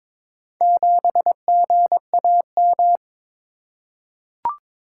Morse code, heard as a keyed tone, about 22 wpm, 700 Hz. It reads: 7GAM
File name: practice / 22wpm / Four Letter-Number Combinations - 22wpm